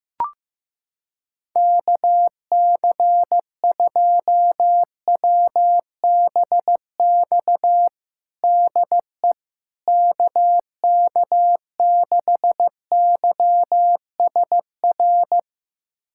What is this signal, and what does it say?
Morse code, heard as a keyed tone, about 15 wpm, 700 Hz. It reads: KC2WBX DE KK6YSR
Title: USA Callsign DE Callsign - 6 Characters - 15wpm